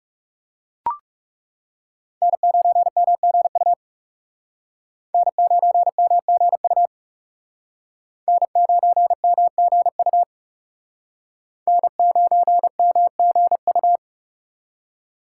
Morse code, heard as a keyed tone, about 38 wpm, 700 Hz. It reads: N9MGU N9MGU N9MGU N9MGU